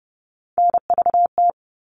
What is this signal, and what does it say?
Morse code, heard as a keyed tone, about 30 wpm, 700 Hz. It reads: NVT
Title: All Cut Numbers - 3-Digit 30wpm